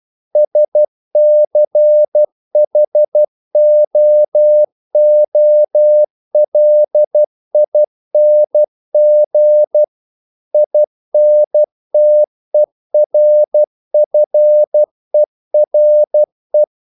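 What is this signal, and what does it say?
Morse code, heard as a keyed tone, about 12 wpm, 600 Hz. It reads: SCHOOLING INTERFERE
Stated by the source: Morse practice sending English quotations.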